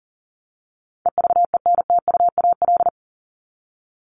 Morse code, heard as a keyed tone, about 40 wpm, 700 Hz. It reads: EVENTUAL
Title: LY Suffix - Top 100 Words with Base Words 40wpm